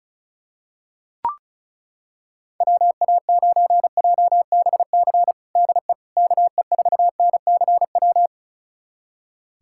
Morse code, heard as a keyed tone, about 36 wpm, 700 Hz. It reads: WA9JBC DE KE4NCW
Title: USA Callsign DE Callsign - 6 Characters - 35wpm